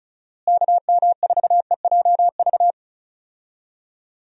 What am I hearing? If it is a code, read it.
Morse code, heard as a keyed tone, about 34 wpm, 700 Hz. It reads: KM4EJV
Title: USA Call Signs - 6 Characters - Repeated 2x - 35wpm